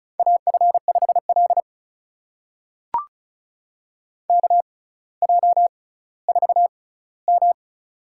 Morse code, heard as a keyed tone, about 36 wpm, spaced as for 18 wpm, 700 Hz. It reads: AF5L KJ4M
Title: USA Call Signs - 4 Characters - 6x Character Spacing - 35wpm